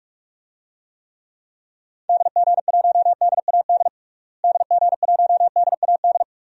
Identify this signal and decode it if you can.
Morse code, heard as a keyed tone, about 43 wpm, 700 Hz. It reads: DG1DAD DG1DAD
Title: DXCC-100 Call Signs - 6 Characters - Repeated 2x - 45wpm